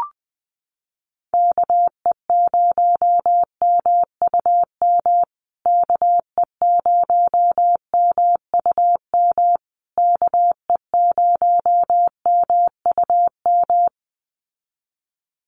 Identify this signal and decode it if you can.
Morse code, heard as a keyed tone, about 19 wpm, 700 Hz. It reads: KE0MUM KE0MUM KE0MUM
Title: USA Call Signs Repeated 3x 20wpm